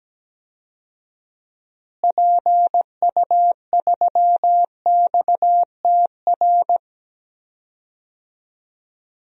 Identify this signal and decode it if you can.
Morse code, heard as a keyed tone, about 17 wpm, 700 Hz. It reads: PU3XTR
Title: DXCC-100 Call Signs - 6 Characters - Repeated 2x - 17wpm